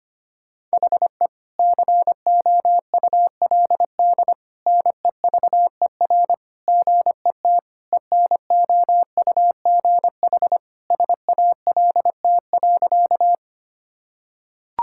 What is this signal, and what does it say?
Morse code, heard as a keyed tone, about 25 wpm, 700 Hz. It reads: HE COULD NEVER GET ENOUGH SALT.
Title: Sentences from Top 1000 Words - Review Part 5 - 25wpm